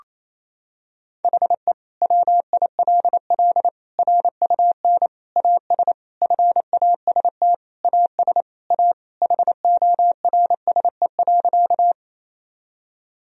Morse code, heard as a keyed tone, about 28 wpm, 700 Hz. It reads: HE WILL RUN AS FAST AS A HORSE.
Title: Sentences from Top 1000 Words - Review Part 2 - 28wpm